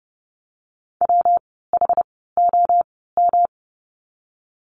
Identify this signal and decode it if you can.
Morse code, heard as a keyed tone, about 30 wpm, spaced as for 17 wpm, 700 Hz. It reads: WHOM
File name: course / 30:10wpm / 115 Lesson 19f - Letter w - Words 30wpm